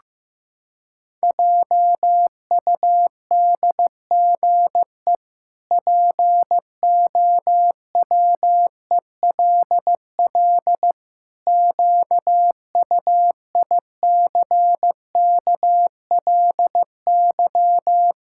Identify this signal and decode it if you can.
Morse code, heard as a keyed tone, about 15 wpm, 700 Hz. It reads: JUDGE POWELL QUICKLY
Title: Pangram Challenge v2 - 15wpm